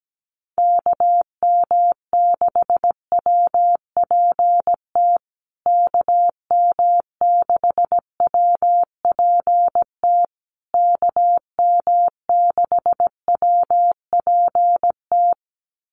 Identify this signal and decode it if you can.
Morse code, heard as a keyed tone, about 17 wpm, 700 Hz. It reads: KM6WPT KM6WPT KM6WPT
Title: USA Call Signs Repeated 3x 17wpm